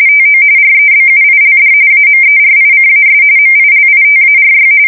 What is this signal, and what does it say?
RTTY, about 45.45 baud. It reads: TO REBUILD AMERICAS MILITARY
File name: fsk170 / battleshipnj